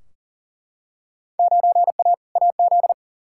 Morse code, heard as a keyed tone, about 40 wpm, 700 Hz. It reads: 9A AZ
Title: Field Day Exchanges - 40wpm